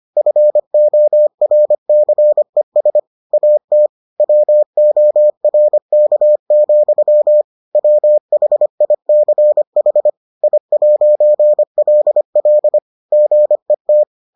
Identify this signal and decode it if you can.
Morse code, heard as a keyed tone, about 25 wpm, 600 Hz. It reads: FORCES AT WORK, WHICH I'LL GET